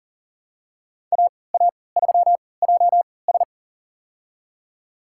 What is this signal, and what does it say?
Morse code, heard as a keyed tone, about 40 wpm, spaced as for 23 wpm, 700 Hz. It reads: AA3JS
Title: USA Call Signs - 3x Character Spacing - 40wpm